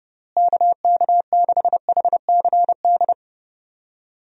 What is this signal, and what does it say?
Morse code, heard as a keyed tone, about 29 wpm, 700 Hz. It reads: KK6HCD
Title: USA Call Signs - 6 Characters - Repeated 3x - 30wpm